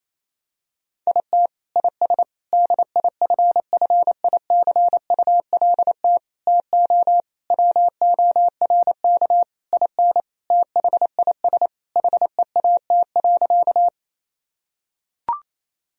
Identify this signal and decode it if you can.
Morse code, heard as a keyed tone, about 28 wpm, 700 Hz. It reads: IT IS DIFFICULT TO WORK IN THIS HEAT.
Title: Sentences from Top 1000 Words - Review Part 3 - 28wpm